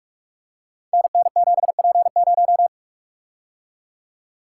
Morse code, heard as a keyed tone, about 44 wpm, 700 Hz. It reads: NN7P0
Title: Code Groups - 5 Letter:Numbers - 45wpm